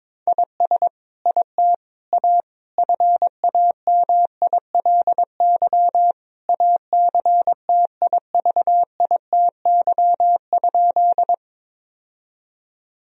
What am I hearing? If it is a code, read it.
Morse code, heard as a keyed tone, about 22 wpm, 700 Hz. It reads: IS IT A FAMILY ACTIVITY?
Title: Sentences from Top 1000 Words - Review Part 4 - 22wpm